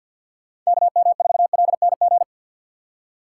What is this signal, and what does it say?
Morse code, heard as a keyed tone, about 53 wpm, 690 Hz. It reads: KM4LNG
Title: USA Call Signs - 6x Character Spacing - 50wpm